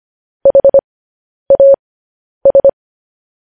Morse code, heard as a keyed tone, about 25 wpm, spaced as for 10 wpm, 550 Hz. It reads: HAS
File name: podcast / News headlines-1x-25x10